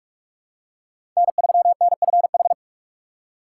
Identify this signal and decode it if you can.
Morse code, heard as a keyed tone, about 45 wpm, 700 Hz. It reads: N3NFH
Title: USA Call Signs - 4x Character Spacing - 45wpm